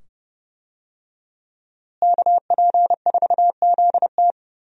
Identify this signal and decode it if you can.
Morse code, heard as a keyed tone, about 30 wpm, 700 Hz. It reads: KP4ZT
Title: USA Call Signs - 5 Characters - Repeated 2x - 30wpm